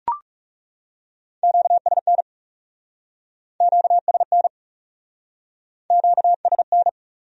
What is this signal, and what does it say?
Morse code, heard as a keyed tone, about 36 wpm, 700 Hz. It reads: QSN QSN QSN
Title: Speed-Racing Q-Codes - 30wpm